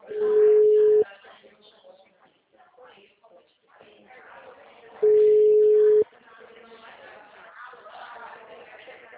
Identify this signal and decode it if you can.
Ringback tone (Europe, 425 Hz), with speech.